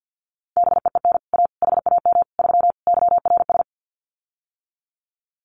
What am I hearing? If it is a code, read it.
Morse code, heard as a keyed tone, about 50 wpm, 700 Hz. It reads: BEEN A HAM 3 YRS
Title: Common QSO Phrases Encore 50wpm